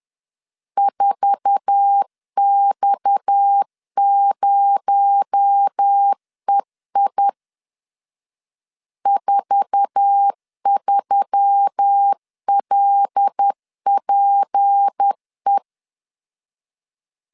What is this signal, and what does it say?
Morse code, heard as a keyed tone, about 10 wpm, 800 Hz. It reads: 4X0EI 43LPE
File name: LEZIONI / 8 YJ45 / 8 YJ4 10wpm 2633